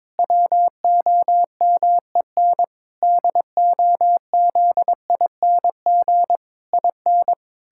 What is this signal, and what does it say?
Morse code, heard as a keyed tone, about 22 wpm, 700 Hz. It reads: WOMEN DOZING IN
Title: Pangram Challenge v2 - 22wpm